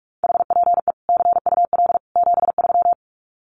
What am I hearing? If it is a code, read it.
Morse code, heard as a keyed tone, about 44 wpm, 700 Hz. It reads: HPE CUL 73
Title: Common QSO Phrases 45wpm